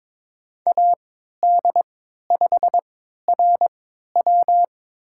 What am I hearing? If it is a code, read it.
Morse code, heard as a keyed tone, about 21 wpm, spaced as for 12 wpm, 700 Hz. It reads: AD5RW
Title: USA Call Signs - 5 Characters - 3x Character Spacing - 22wpm